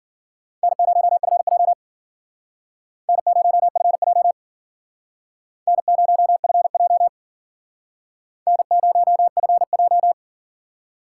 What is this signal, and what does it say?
Morse code, heard as a keyed tone, about 52 wpm, 700 Hz. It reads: N0FJ N0FJ N0FJ N0FJ